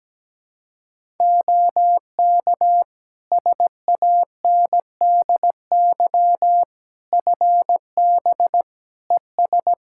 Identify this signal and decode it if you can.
Morse code, heard as a keyed tone, about 17 wpm, 700 Hz. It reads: OK SANDY FB ES
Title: K1USN SST QSO Practice 17wpm